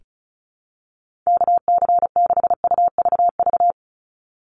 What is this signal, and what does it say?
Morse code, heard as a keyed tone, about 36 wpm, 700 Hz. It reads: KC6UVV
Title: USA Call Signs 35wpm